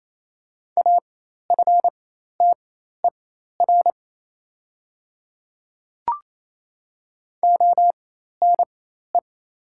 Morse code, heard as a keyed tone, about 27 wpm, spaced as for 13 wpm, 700 Hz. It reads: AFTER ONE
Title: 4x Character Spacing - Top 200 Words - 28wpm